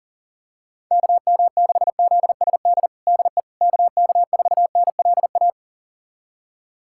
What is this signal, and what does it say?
Morse code, heard as a keyed tone, about 40 wpm, 700 Hz. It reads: KM6ZSD DE KK4NLA
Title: USA Callsign DE Callsign - 6 Characters - 40wpm